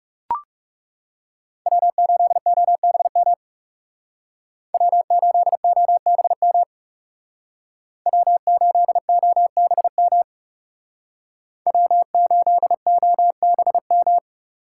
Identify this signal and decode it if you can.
Morse code, heard as a keyed tone, about 38 wpm, 700 Hz. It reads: W8OBM W8OBM W8OBM W8OBM